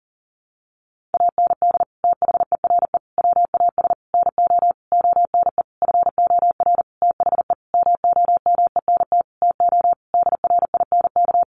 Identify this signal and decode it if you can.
Morse code, heard as a keyed tone, about 40 wpm, 700 Hz. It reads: AND THERE WAS NO ONE FOR THE MOMENT TO DRINK
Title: Sentences from Top 1500 Words 40wpm